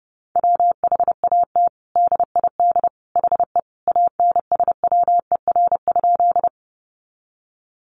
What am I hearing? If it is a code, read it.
Morse code, heard as a keyed tone, about 30 wpm, 700 Hz. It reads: WHAT DID HE ANSWER?
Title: Sentences from Top 300 Words 30wpm